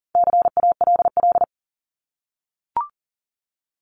Morse code, heard as a keyed tone, about 40 wpm, 700 Hz. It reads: CALL
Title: Top 300 Words 40wpm